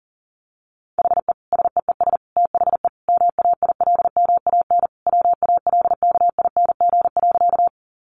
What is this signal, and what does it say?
Morse code, heard as a keyed tone, about 40 wpm, 700 Hz. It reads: HE SEES THE MAILMAN WALKING.
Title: Additional Easy Phrases 40wpm